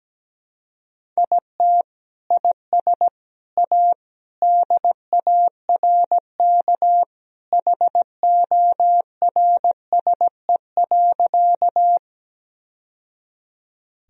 Morse code, heard as a keyed tone, about 17 wpm, 700 Hz. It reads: IT IS A DARK HORSE.
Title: Sentences from Top 1000 Words - Review Part 1 - 17wpm